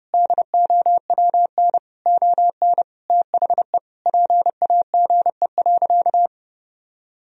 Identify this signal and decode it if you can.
Morse code, heard as a keyed tone, about 29 wpm, 700 Hz. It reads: DOWN ON THE PAGE.
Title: Sentences from Top 1000 Words - Review Part 1 - 30wpm